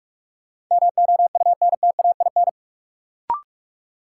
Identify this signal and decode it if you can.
Morse code, heard as a keyed tone, about 46 wpm, 700 Hz. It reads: MOUNTAIN
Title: Sentences from Top 1000 Words - Review Part 1 - New Word Drill - 45wpm